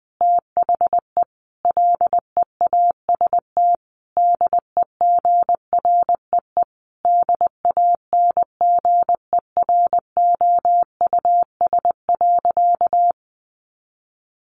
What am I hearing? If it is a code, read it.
Morse code, heard as a keyed tone, about 20 wpm, 700 Hz. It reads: THE LEAST DEGREE DANGEROUS.
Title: Sentences from Top 1500 Words 20wpm